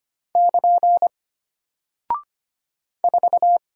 Morse code, heard as a keyed tone, about 25 wpm, 700 Hz. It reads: <KN> 4
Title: Single Character (Letter:Number:Punctuation:Prosign) Alt 25wpm